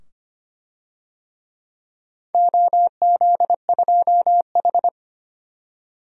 Morse code, heard as a keyed tone, about 25 wpm, 700 Hz. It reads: OZ2H
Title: DXCC-100 Call Signs - 4 Characters - 25wpm